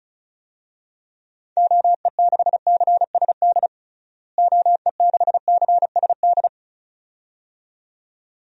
Morse code, heard as a keyed tone, about 36 wpm, 700 Hz. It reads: OE6CSD OE6CSD
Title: DXCC-100 Call Signs - 6 Characters - Repeated 2x - 35wpm